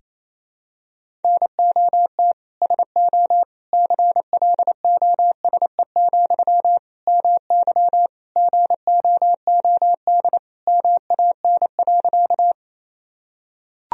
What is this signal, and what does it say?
Morse code, heard as a keyed tone, about 28 wpm, 700 Hz. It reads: NOT SO CLOSE, MY GOOD MAN.
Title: Sentences from Top 1000 Words - Review Part 1 - 28wpm